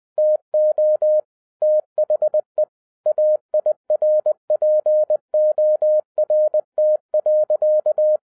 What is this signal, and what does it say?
Morse code, heard as a keyed tone, about 20 wpm, 610 Hz. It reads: TO THE AIRPORT.